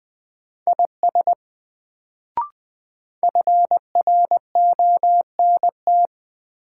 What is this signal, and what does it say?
Morse code, heard as a keyed tone, about 20 wpm, 700 Hz. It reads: IS FRONT
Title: Sets of 2 Words - New Word Drill - 20wpm